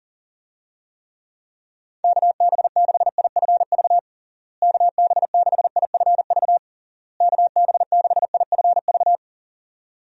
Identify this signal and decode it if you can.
Morse code, heard as a keyed tone, about 40 wpm, 700 Hz. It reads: KB6IFV KB6IFV KB6IFV
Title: USA Call Signs - 6 Characters - Repeated 3x - 40wpm